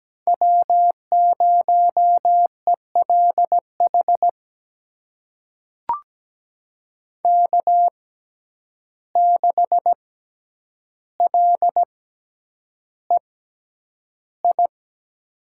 Morse code, heard as a keyed tone, about 17 wpm, spaced as for 9 wpm, 700 Hz. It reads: W0ELH K6LEI
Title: USA Call Signs - 5 Characters - 6x Character Spacing - 17wpm